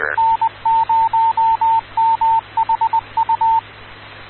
Morse code, heard as a keyed tone, about 20 wpm, 900 Hz, with radio noise. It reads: N0MHU
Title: Rocky Mountain Radio League Repeaters